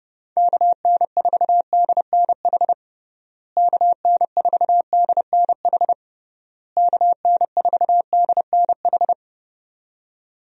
Morse code, heard as a keyed tone, about 30 wpm, 700 Hz. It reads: KN4DNH KN4DNH KN4DNH
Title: USA Call Signs - 6 Characters - Repeated 3x - 30wpm